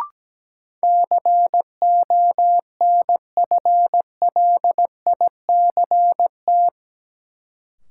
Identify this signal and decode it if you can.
Morse code, heard as a keyed tone, about 17 wpm, 700 Hz. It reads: CONFLICT
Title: Sentences from Top 1300 Words - New Word Drill - 17wpm